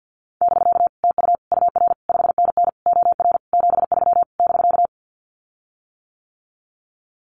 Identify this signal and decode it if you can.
Morse code, heard as a keyed tone, about 48 wpm, 700 Hz. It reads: <BK> TU UR 5NN OR 73 <BK>
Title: Common QSO Phrases Encore 50wpm